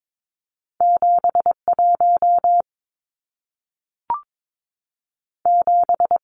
Morse code, heard as a keyed tone, about 22 wpm, 700 Hz. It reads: 7 1 7
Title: Numbers - 2-Digit 22wpm